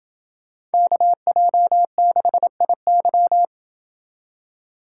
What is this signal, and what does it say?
Morse code, heard as a keyed tone, about 29 wpm, 700 Hz. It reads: KJ6IY